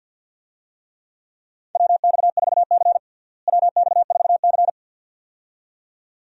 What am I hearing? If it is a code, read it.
Morse code, heard as a keyed tone, about 47 wpm, 700 Hz. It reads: WX4C WX4C
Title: USA Call Signs - 4 Characters - Repeated 2x - 50wpm